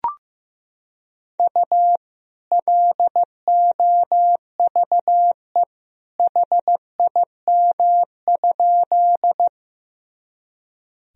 Morse code, heard as a keyed tone, about 16 wpm, 700 Hz. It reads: U LOVE HIM?
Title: Sentences from Top 400 Words and Abbreviations 15wpm